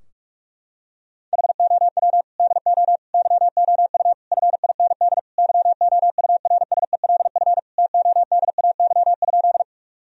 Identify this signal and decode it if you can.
Morse code, heard as a keyed tone, about 46 wpm, 700 Hz. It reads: HOW DO YOU FIND YOURSELF TODAY?
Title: Sentences from Top 1000 Words - Review Part 4 - 45wpm